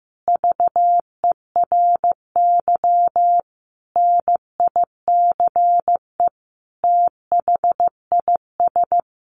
Morse code, heard as a keyed tone, about 15 wpm, 700 Hz. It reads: VERY NICE THIS
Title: Sentences from Top 1500 Words 15wpm